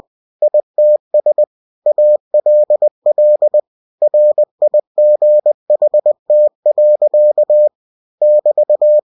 Morse code, heard as a keyed tone, about 20 wpm, 600 Hz. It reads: ITS ALL RIGHT. <BT>
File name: cw-books / winnie-the-pooh / winnie-20wpm-page-0088